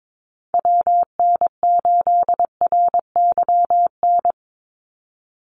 Morse code, heard as a keyed tone, about 22 wpm, 700 Hz. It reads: WN8RYN
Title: USA Call Signs 22wpm